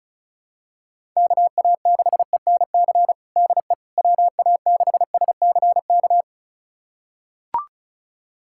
Morse code, heard as a keyed tone, about 36 wpm, 700 Hz. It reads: KA6ENC DE WA6SCK
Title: USA Callsign DE Callsign - 6 Characters - 35wpm